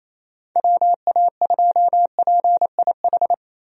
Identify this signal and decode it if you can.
Morse code, heard as a keyed tone, about 28 wpm, 700 Hz. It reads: WA2PIH
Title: USA Call Signs - 6 Characters - Repeated 3x - 28wpm